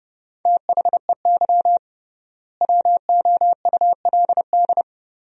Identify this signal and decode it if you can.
Morse code, heard as a keyed tone, about 30 wpm, 700 Hz. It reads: THEY WOULD
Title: 3x Word-Spacing - Sentences from Top 500 Words - 30wpm